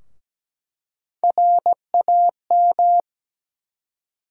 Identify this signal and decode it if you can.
Morse code, heard as a keyed tone, about 17 wpm, 700 Hz. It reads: RAM
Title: USA Call Sign Postfixes - 3 Letters 17wpm